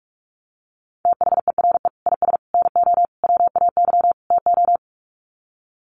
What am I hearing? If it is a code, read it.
Morse code, heard as a keyed tone, about 44 wpm, 700 Hz. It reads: THERE IS NO WAY TO
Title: Sets of 5 Words Encore 45wpm